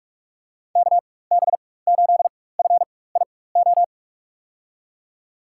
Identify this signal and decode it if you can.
Morse code, heard as a keyed tone, about 46 wpm, spaced as for 21 wpm, 700 Hz. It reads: KB8FIO
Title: USA Call Signs - 6 Characters - 4x Character Spacing - 45wpm